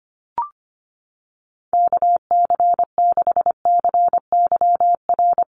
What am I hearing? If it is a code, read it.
Morse code, heard as a keyed tone, about 25 wpm, 700 Hz. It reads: KC6CYR
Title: USA Call Signs Repeated 2x 25wpm